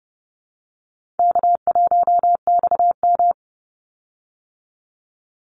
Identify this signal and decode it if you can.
Morse code, heard as a keyed tone, about 30 wpm, 700 Hz. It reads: K1XM
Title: Easy Call Signs Repeated 2x - 30wpm